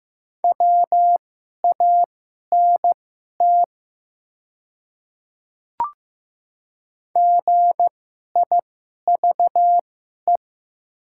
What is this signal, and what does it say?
Morse code, heard as a keyed tone, about 15 wpm, spaced as for 11 wpm, 700 Hz. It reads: WANT GIVE
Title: 2x Character Spacing - Top 200 Words - 15wpm